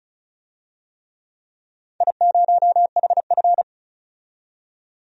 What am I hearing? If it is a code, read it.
Morse code, heard as a keyed tone, about 36 wpm, 700 Hz. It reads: I0HF
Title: DXCC-100 Call Signs - 4 Characters - 35wpm